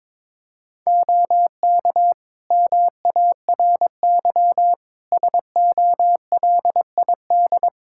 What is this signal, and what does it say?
Morse code, heard as a keyed tone, about 22 wpm, 700 Hz. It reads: OK MARY SOLID
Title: Practice QSO - Exchange 3 - More Info - 22wpm